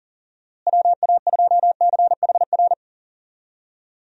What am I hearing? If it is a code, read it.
Morse code, heard as a keyed tone, about 40 wpm, 700 Hz. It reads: WA2CHR
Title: USA Call Signs - 6 Characters - Repeated 2x - 40wpm